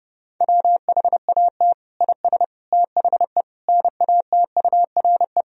Morse code, heard as a keyed tone, about 30 wpm, 700 Hz. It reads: WHAT IS THE NATURE
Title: Sentences from Top 1000 Words - Review Part 3 - 30wpm